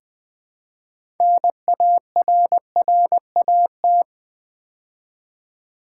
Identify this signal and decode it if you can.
Morse code, heard as a keyed tone, about 20 wpm, 700 Hz. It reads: NARRAT
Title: Build A Word - 3 Letters At A Time - 9-Letter Words - 20wpm